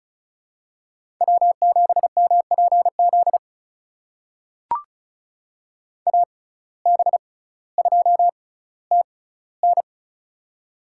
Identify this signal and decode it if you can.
Morse code, heard as a keyed tone, about 36 wpm, spaced as for 18 wpm, 700 Hz. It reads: W7MPZ AB2TN
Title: USA Call Signs - 5 Characters - 6x Character Spacing - 35wpm